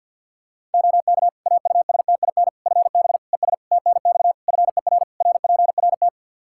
Morse code, heard as a keyed tone, about 52 wpm, 700 Hz. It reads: OK AUSTIN FB ES TNX FER RPRT
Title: Practice QSO - Exchange 2 - Info - 50wpm